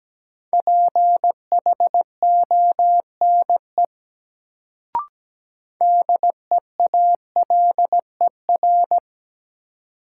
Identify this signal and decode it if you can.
Morse code, heard as a keyed tone, about 17 wpm, 700 Hz. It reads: PHONE DEALER
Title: Sentences from Top 1100 Words - New Word Drill Alt - 17wpm